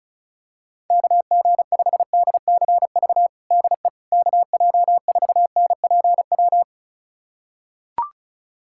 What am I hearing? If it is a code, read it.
Morse code, heard as a keyed tone, about 34 wpm, 700 Hz. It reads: KG5DCV DE KJ4NPW
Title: USA Callsign DE Callsign - 6 Characters - 35wpm